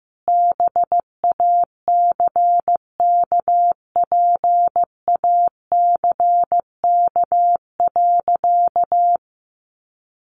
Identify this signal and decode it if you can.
Morse code, heard as a keyed tone, about 15 wpm, 700 Hz. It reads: BACKPACK.
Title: Additional Easy Phrases 15wpm